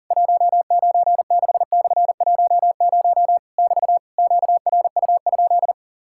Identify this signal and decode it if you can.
Morse code, heard as a keyed tone, about 40 wpm, 700 Hz. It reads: 196/10 <BT> QRU?
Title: Worked All Europe DX Contest - QTC Exchange - 40wpm